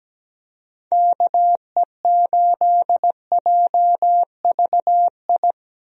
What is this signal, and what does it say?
Morse code, heard as a keyed tone, about 17 wpm, 700 Hz. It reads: KE8JVI
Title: USA Call Signs - 6 Characters - Repeated 2x - 17wpm